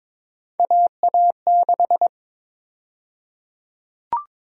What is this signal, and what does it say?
Morse code, heard as a keyed tone, about 21 wpm, 700 Hz. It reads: AA6
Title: USA Call Sign Prefixes - 22wpm